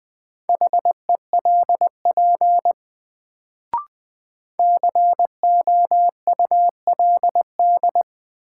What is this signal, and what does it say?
Morse code, heard as a keyed tone, about 21 wpm, 700 Hz. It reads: HELP COULD
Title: Sets of 2 Words - New Word Drill - 20wpm